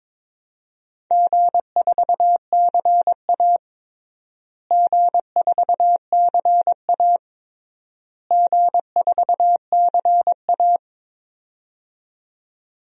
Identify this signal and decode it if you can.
Morse code, heard as a keyed tone, about 21 wpm, 700 Hz. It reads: G4CA G4CA G4CA
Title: DXCC-100 Call Signs - 4 Characters - Repeated 3x - 22wpm